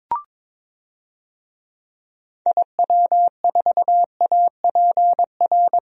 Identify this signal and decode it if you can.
Morse code, heard as a keyed tone, about 22 wpm, 700 Hz. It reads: IW4APR
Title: DXCC-100 Call Signs - 6 Characters - 22wpm